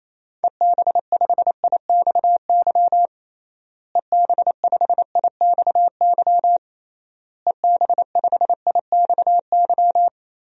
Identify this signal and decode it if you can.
Morse code, heard as a keyed tone, about 28 wpm, 700 Hz. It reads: EB5IXY EB5IXY EB5IXY
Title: DXCC-100 Call Signs - 6 Characters - Repeated 3x - 28wpm